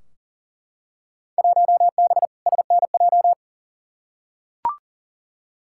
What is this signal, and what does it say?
Morse code, heard as a keyed tone, about 40 wpm, 700 Hz. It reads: 1B SNJ
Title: Field Day Exchanges - 40wpm